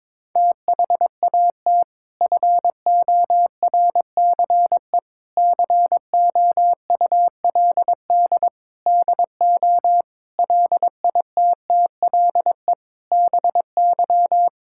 Morse code, heard as a keyed tone, about 22 wpm, 700 Hz. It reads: THAT FORCE COULD DO LITTLE BY
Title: Sentences from Top 1000 Words - Review Part 5 - 22wpm